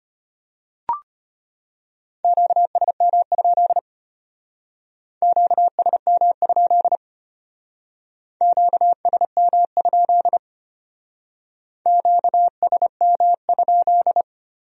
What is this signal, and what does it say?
Morse code, heard as a keyed tone, about 31 wpm, 700 Hz. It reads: QSM? QSM? QSM? QSM?